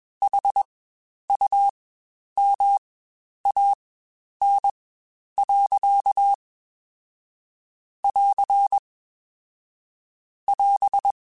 Morse code, heard as a keyed tone, about 22 wpm, spaced as for 10 wpm, 800 Hz. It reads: HUMAN. <AR> <AS>
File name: Spaced / Chuckles Spaced-mp3s / HumanErr sp @20